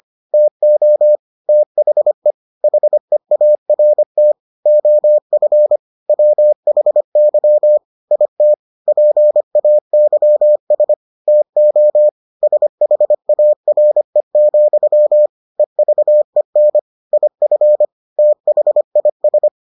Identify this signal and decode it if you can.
Morse code, heard as a keyed tone, about 25 wpm, 600 Hz. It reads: TO THE HEART OF WHY IT PAYS TO SHARE, EVEN IF THIS